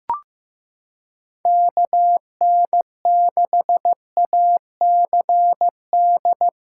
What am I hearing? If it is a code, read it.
Morse code, heard as a keyed tone, about 15 wpm, 700 Hz. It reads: KN6ACD
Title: USA Callsign DE Callsign 15wpm